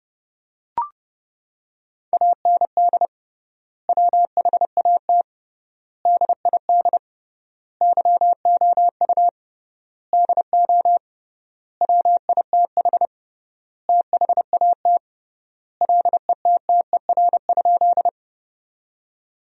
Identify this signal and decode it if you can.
Morse code, heard as a keyed tone, about 30 wpm, 700 Hz. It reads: AND WHAT DID YOU DO WITH THAT LETTER?
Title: 3x Word-Spacing - Sentences from Top 300 Words - 30wpm